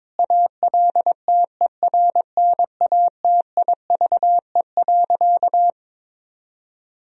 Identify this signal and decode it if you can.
Morse code, heard as a keyed tone, about 22 wpm, 700 Hz. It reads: ALTERNATIVE.
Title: Sentences from Top 1000 Words - Review Part 4 - 22wpm